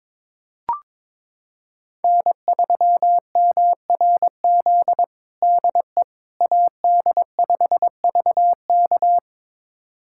Morse code, heard as a keyed tone, about 22 wpm, 700 Hz. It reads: N3MRZ DE AD5VK
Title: USA Callsign DE Callsign - 5 Characters - 22wpm